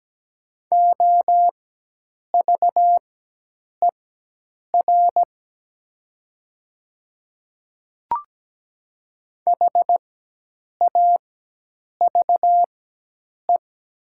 Morse code, heard as a keyed tone, about 17 wpm, spaced as for 8 wpm, 700 Hz. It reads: OVER HAVE
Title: 4x Character Spacing - Top 100 Words - 17wpm